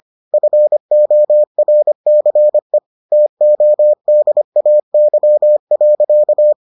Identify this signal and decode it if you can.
Morse code, heard as a keyed tone, about 25 wpm, 600 Hz. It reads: FORCE TODAY.